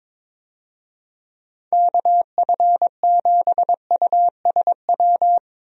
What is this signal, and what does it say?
Morse code, heard as a keyed tone, about 22 wpm, 700 Hz. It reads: KF7USW
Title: USA Call Signs - 6 Characters - 22wpm